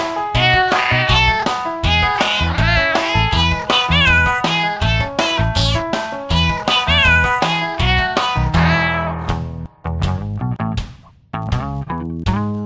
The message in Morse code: none

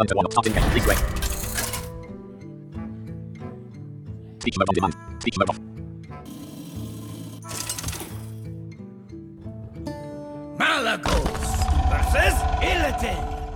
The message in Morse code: none

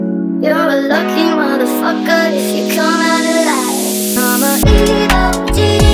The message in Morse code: none

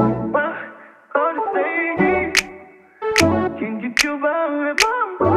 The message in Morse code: none

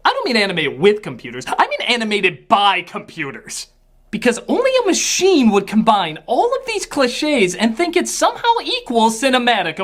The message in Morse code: none

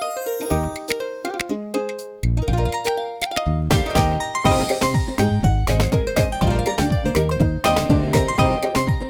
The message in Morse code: none

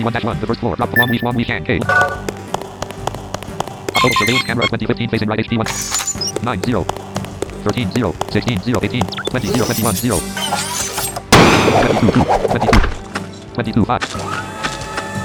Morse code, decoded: none

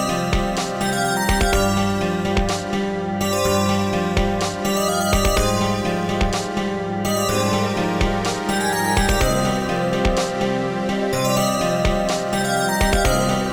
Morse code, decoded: none

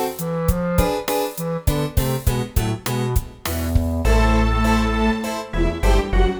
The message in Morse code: none

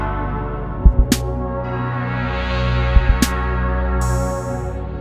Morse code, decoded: none

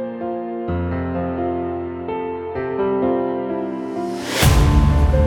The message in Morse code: none

